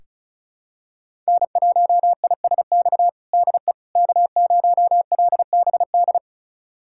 Morse code, heard as a keyed tone, about 34 wpm, 700 Hz. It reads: N1ISX DE K0LBD
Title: USA Callsign DE Callsign - 5 Characters - 35wpm